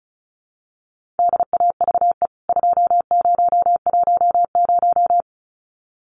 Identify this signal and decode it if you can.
Morse code, heard as a keyed tone, about 34 wpm, 700 Hz. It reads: DAVE 2010
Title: CWT Exchange 35wpm